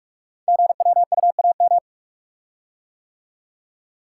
Morse code, heard as a keyed tone, about 44 wpm, 700 Hz. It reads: GWUAM
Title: Code Groups - 5 Letters - 45wpm